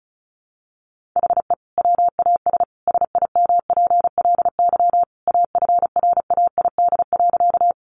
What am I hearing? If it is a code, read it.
Morse code, heard as a keyed tone, about 34 wpm, 700 Hz. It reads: HE WAS SIMPLY AFRAID.
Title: Sentences from Top 1500 Words 35wpm